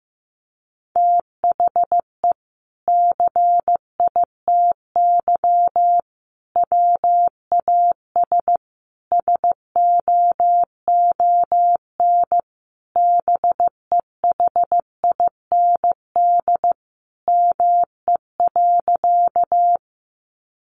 Morse code, heard as a keyed tone, about 15 wpm, 700 Hz. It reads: THE CITY WAS SOON BEHIND ME.
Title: Sentences from Top 500 Words 15wpm